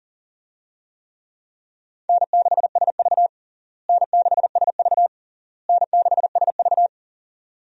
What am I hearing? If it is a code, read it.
Morse code, heard as a keyed tone, about 40 wpm, 700 Hz. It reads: N6SV N6SV N6SV
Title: USA Call Signs - 4 Characters - Repeated 3x - 40wpm